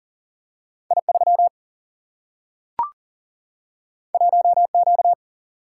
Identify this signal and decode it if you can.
Morse code, heard as a keyed tone, about 40 wpm, 700 Hz. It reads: I3 1Q
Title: Single-Letter Single-Number 40wpm